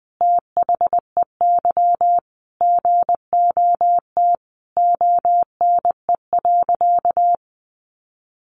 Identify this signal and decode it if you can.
Morse code, heard as a keyed tone, about 19 wpm, 700 Hz. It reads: THEY GOT ONE.
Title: Sentences from Top 300 Words 20wpm